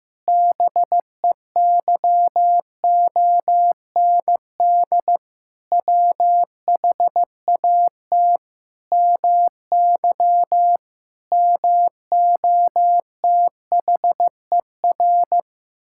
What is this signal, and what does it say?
Morse code, heard as a keyed tone, about 15 wpm, 700 Hz. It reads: BEYOND WHAT MY MOTHER
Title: Sentences from Top 1000 Words - Review Part 6 - 15wpm